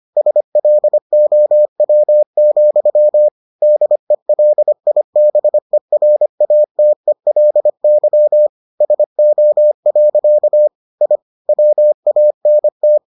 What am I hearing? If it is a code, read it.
Morse code, heard as a keyed tone, about 25 wpm, 600 Hz. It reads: SLOW, DELIBERATELY SO. I WANT